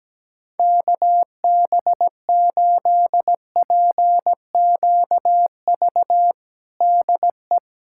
Morse code, heard as a keyed tone, about 17 wpm, 700 Hz. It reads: KB8PQV DE
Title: USA Callsign DE Callsign - 6 Characters - 17wpm